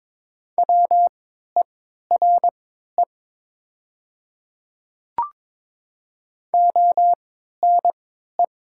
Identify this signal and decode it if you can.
Morse code, heard as a keyed tone, about 22 wpm, spaced as for 13 wpm, 700 Hz. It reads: WERE ONE
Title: 3x Character Spacing - Top 100 Words - 22wpm